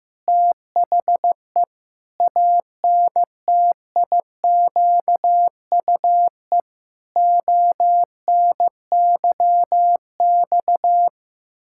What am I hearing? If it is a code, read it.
Morse code, heard as a keyed tone, about 15 wpm, 700 Hz. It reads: THE ANTIQUE ONYX
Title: Pangram Challenge v2 - 15wpm